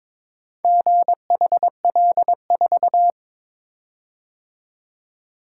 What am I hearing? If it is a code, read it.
Morse code, heard as a keyed tone, about 23 wpm, 700 Hz. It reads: GHL4
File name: practice / 22wpm / Four Letter-Number Combinations - 22wpm